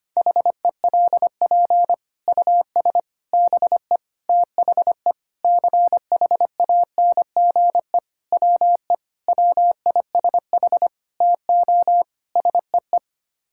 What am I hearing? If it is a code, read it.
Morse code, heard as a keyed tone, about 25 wpm, 700 Hz. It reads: HELP US BE THE CHANGE WE WISH TO SEE
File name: Sets of 9 Words - 25wpm